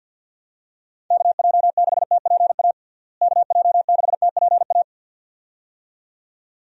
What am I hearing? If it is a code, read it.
Morse code, heard as a keyed tone, about 51 wpm, 700 Hz. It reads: KJ6TPA KJ6TPA
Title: USA Call Signs - 6 Characters - Repeated 2x - 50wpm